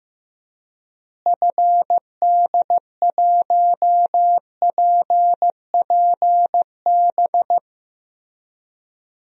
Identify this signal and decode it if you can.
Morse code, heard as a keyed tone, about 16 wpm, 700 Hz. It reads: FD1PPB
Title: DXCC-100 Call Signs - 6 Characters - Repeated 3x - 15wpm